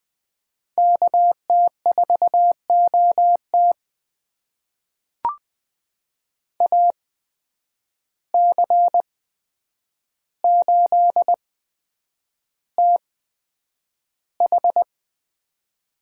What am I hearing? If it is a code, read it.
Morse code, heard as a keyed tone, about 20 wpm, spaced as for 9 wpm, 700 Hz. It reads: KT4OT AC8TH
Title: USA Call Signs - 5 Characters - 8x Character Spacing - 20wpm